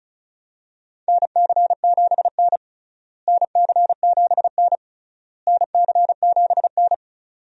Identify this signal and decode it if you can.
Morse code, heard as a keyed tone, about 34 wpm, 700 Hz. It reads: NC7N NC7N NC7N
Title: USA Call Signs - 4 Characters - Repeated 3x - 35wpm